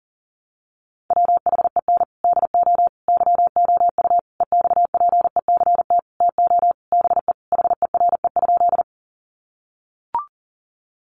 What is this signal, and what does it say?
Morse code, heard as a keyed tone, about 40 wpm, 700 Hz. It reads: WHEN DO YOU EXPECT TO BE HERE?
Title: Sentences from Top 1100 Words 40wpm